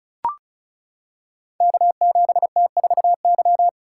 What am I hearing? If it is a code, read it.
Morse code, heard as a keyed tone, about 36 wpm, 700 Hz. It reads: K7T4Y
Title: Code Groups - 5 Letter:Numbers - 35wpm